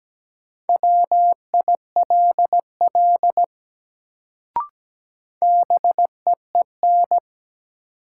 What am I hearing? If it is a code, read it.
Morse code, heard as a keyed tone, about 17 wpm, 700 Hz. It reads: WILL BEEN
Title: Sentences from Top 100 Words - New Word Drill - 17wpm